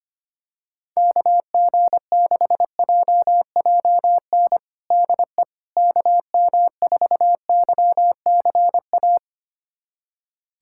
Morse code, heard as a keyed tone, about 25 wpm, 700 Hz. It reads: KG6JJN DE KM4YCA
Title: USA Callsign DE Callsign - 6 Characters - 25wpm